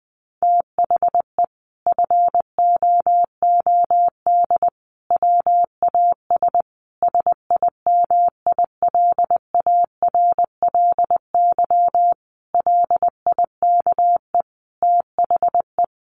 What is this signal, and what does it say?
Morse code, heard as a keyed tone, about 20 wpm, 700 Hz. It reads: THE FOOD WAS SIMILARLY LIKE THE
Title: Sentences from Top 1400 Words 20wpm